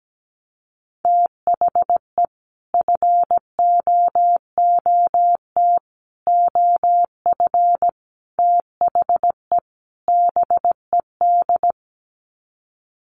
Morse code, heard as a keyed tone, about 17 wpm, 700 Hz. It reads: THE FOOT OF THE BED
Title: Sets of 5 Words Encore2 17wpm